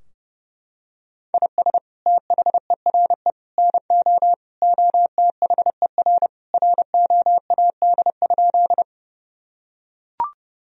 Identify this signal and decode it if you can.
Morse code, heard as a keyed tone, about 30 wpm, 700 Hz. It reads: IS THERE NO OTHER ROAD?
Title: Sentences from Top 1000 Words - Review Part 2 - 30wpm